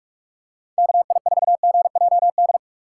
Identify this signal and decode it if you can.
Morse code, heard as a keyed tone, about 46 wpm, 700 Hz. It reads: KI4GJD
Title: USA Call Signs - 6 Characters - 6x Character Spacing - 45wpm